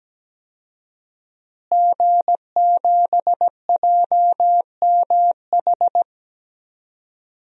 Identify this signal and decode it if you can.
Morse code, heard as a keyed tone, about 17 wpm, 700 Hz. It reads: G7JMH
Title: DXCC-100 Call Signs - 5 Characters - Repeated 3x - 17wpm